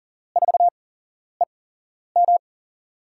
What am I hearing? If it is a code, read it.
Morse code, heard as a keyed tone, about 40 wpm, spaced as for 11 wpm, 700 Hz. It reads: 4EM
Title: USA Call Signs - 8x Character Spacing - 40wpm